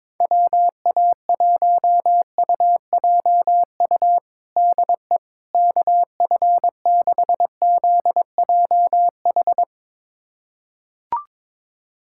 Morse code, heard as a keyed tone, about 22 wpm, 700 Hz. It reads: WA1UJU DE KF6ZJH
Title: USA Callsign DE Callsign 22wpm